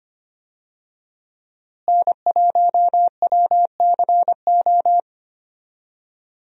Morse code, heard as a keyed tone, about 25 wpm, 700 Hz. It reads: N1WCO
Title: USA Call Signs - 5 Characters - Repeated 3x - 25wpm